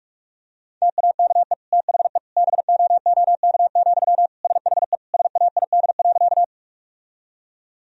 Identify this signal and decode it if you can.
Morse code, heard as a keyed tone, about 46 wpm, 700 Hz. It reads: TAKE THE BOOK, SHE SAID.
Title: Sentences from Top 1000 Words - Review Part 1 - 45wpm